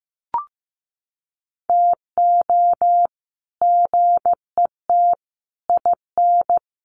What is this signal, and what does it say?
Morse code, heard as a keyed tone, about 15 wpm, 700 Hz. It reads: TO GET IN
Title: Sets of 5 Words Encore2 15wpm